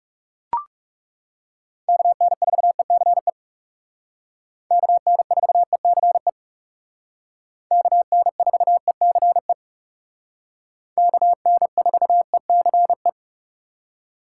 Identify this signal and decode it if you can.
Morse code, heard as a keyed tone, about 36 wpm, 700 Hz. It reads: KN4ECE KN4ECE KN4ECE KN4ECE
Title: Speed Racing - USA 2x3 Call Signs 30wpm